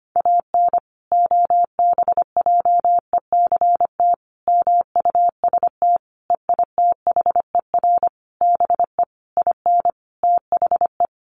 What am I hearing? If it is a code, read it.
Morse code, heard as a keyed tone, about 25 wpm, 700 Hz. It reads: AN OBJECT MUST EITHER BE IN THE
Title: Sentences from Top 1400 Words 25wpm